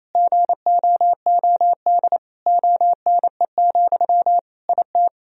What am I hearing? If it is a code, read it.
Morse code, heard as a keyed tone, about 28 wpm, 700 Hz. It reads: GOOD ONE, IT
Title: Sentences from Top 1000 Words - Review Part 1 - 28wpm